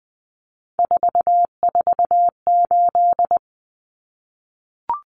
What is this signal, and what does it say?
Morse code, heard as a keyed tone, about 20 wpm, 700 Hz. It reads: 448
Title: Common Cut Numbers - 3-Digit 20wpm